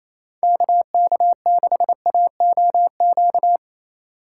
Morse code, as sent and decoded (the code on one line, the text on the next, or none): -.- -.- -.... .- --- --.-
KK6AOQ